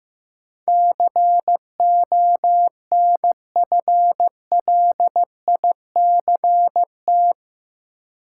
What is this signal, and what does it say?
Morse code, heard as a keyed tone, about 16 wpm, 700 Hz. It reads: CONFLICT
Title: Sentences from Top 1300 Words - New Word Drill Alt - 15wpm